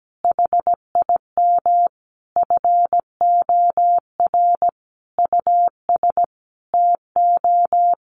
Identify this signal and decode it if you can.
Morse code, heard as a keyed tone, about 17 wpm, 700 Hz. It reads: HIM FOR US TO
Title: Sentences from Top 500 Words 17wpm